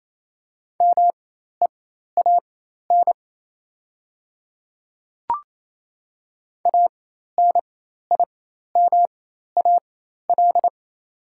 Morse code, heard as a keyed tone, about 28 wpm, spaced as for 13 wpm, 690 Hz. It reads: MEAN ANIMAL
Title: 4x Character Spacing - Top 500 Words - 28wpm